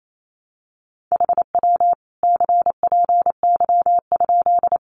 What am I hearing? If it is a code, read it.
Morse code, heard as a keyed tone, about 28 wpm, 700 Hz. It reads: HW CPY?
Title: Common QSO Phrases 28wpm